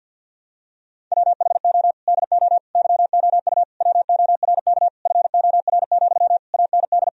Morse code, heard as a keyed tone, about 50 wpm, 700 Hz. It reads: WHO DO YOU WORK FOR, AND
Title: Sentences from Top 1000 Words - Review Part 4 - 50wpm